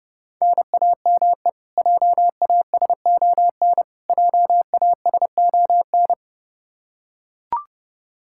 Morse code, heard as a keyed tone, about 30 wpm, 700 Hz. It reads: NAME JASON JASON
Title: K1USN SST QSO Practice 30wpm